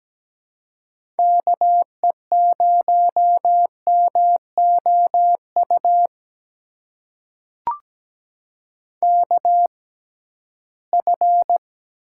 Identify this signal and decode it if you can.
Morse code, heard as a keyed tone, about 17 wpm, spaced as for 13 wpm, 700 Hz. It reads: KE0MOU KF
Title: USA Call Signs - 6 Characters - 6x Character Spacing - 17wpm